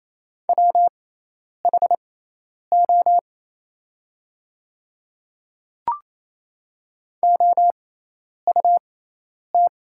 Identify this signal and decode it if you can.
Morse code, heard as a keyed tone, about 27 wpm, spaced as for 10 wpm, 700 Hz. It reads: WHO OUT
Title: 6x Character Spacing - Top 300 Words - 28wpm